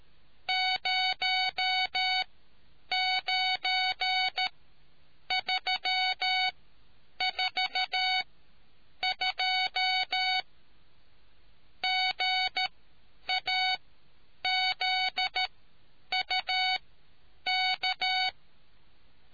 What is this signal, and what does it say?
Morse code, heard as a keyed tone, about 13 wpm, spaced as for 8 wpm, 2250 Hz. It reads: 09342 GAZUK